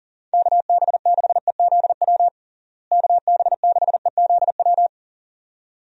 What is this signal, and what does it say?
Morse code, heard as a keyed tone, about 40 wpm, 700 Hz. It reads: KB6EZW KB6EZW
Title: USA Call Signs - 6 Characters - Repeated 2x - 40wpm